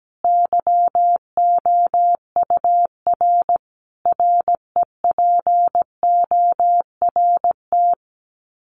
Morse code, heard as a keyed tone, about 17 wpm, 700 Hz. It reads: YOUR REPORT
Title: Common QSO Phrases 17wpm